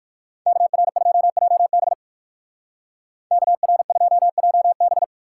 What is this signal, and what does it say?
Morse code, heard as a keyed tone, about 53 wpm, 700 Hz. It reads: KR2JB KR2JB
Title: Speed Racing - USA 2x2 Call Signs 45wpm